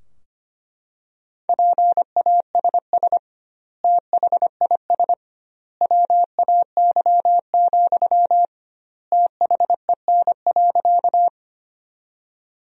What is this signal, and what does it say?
Morse code, heard as a keyed tone, about 25 wpm, 700 Hz. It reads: PASS THIS WAY, THEN.
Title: 2x Word-Spacing - Sentences from Top 400 Words - 25wpm